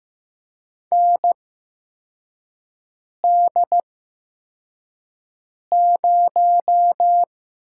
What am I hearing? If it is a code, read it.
Morse code, heard as a keyed tone, about 15 wpm, spaced as for 4 wpm, 700 Hz. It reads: ND0
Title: USA Call Signs - 4 Characters - 8x Character Spacing - 15wpm